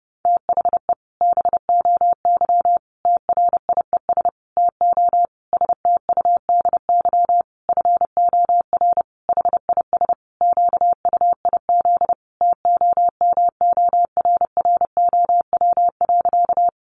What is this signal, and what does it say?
Morse code, heard as a keyed tone, about 30 wpm, 700 Hz. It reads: THE BOY TRIES TO STUDY FOR HIS QUIZ TOMORROW.
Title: Additional Easy Phrases 30wpm